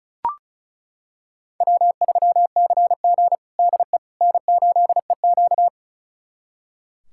Morse code, heard as a keyed tone, about 36 wpm, 700 Hz. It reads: W3CG DE N8EQ
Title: USA Callsign DE Callsign - 4 Characters - 35wpm